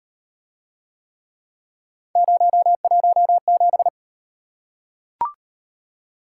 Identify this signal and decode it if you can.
Morse code, heard as a keyed tone, about 33 wpm, 700 Hz. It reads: 017